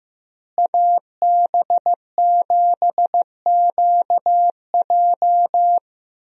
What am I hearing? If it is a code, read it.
Morse code, heard as a keyed tone, about 15 wpm, 700 Hz. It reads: AB7QJ
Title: USA Call Signs - 5 Characters - Repeated 2x - 15wpm